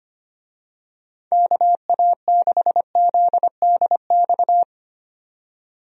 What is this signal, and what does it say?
Morse code, heard as a keyed tone, about 25 wpm, 700 Hz. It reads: KA6ZDX
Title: USA Call Signs - 6 Characters - 25wpm